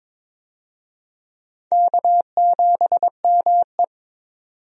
Morse code, heard as a keyed tone, about 21 wpm, 700 Hz. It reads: K7ME
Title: USA Call Signs - 4 Characters - 22wpm